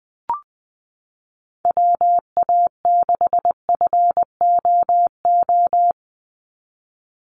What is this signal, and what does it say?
Morse code, heard as a keyed tone, about 19 wpm, 700 Hz. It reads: WA6FOO
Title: USA Call Signs 20wpm